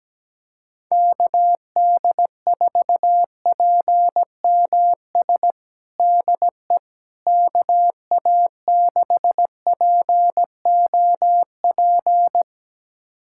Morse code, heard as a keyed tone, about 17 wpm, 700 Hz. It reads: KD4PMS DE KA6POP
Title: USA Callsign DE Callsign 17wpm